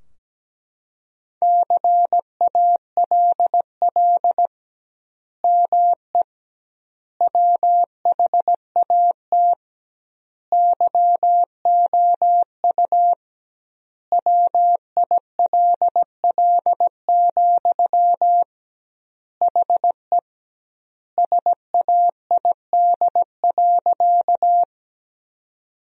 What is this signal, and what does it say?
Morse code, heard as a keyed tone, about 17 wpm, 700 Hz. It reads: CALL ME WHAT YOU WILL, HE SAID.
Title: 2x Word-Spacing - Sentences from Top 200 Words - 17wpm